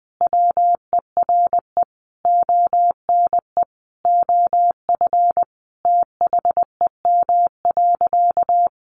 Morse code, heard as a keyed tone, about 20 wpm, 700 Hz. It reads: WERE ONE OF THEM.
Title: Sentences from Top 1500 Words 20wpm